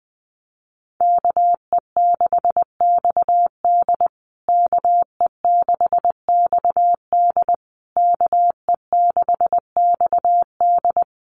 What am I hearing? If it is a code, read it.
Morse code, heard as a keyed tone, about 20 wpm, 700 Hz. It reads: KE6XD KE6XD KE6XD
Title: USA Call Signs Repeated 3x 20wpm